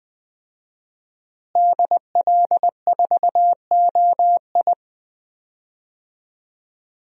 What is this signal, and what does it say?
Morse code, heard as a keyed tone, about 20 wpm, 700 Hz. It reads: DL4OI
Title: USA Call Signs - 5 Characters - Repeated 2x - 20wpm